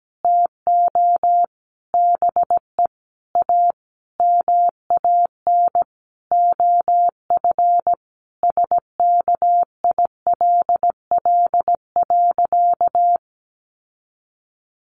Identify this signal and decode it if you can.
Morse code, heard as a keyed tone, about 17 wpm, 700 Hz. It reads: TO BE A MAN OF SKILL.
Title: Sentences from Top 1200 Words 17wpm